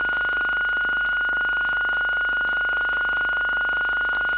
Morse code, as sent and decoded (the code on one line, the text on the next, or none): none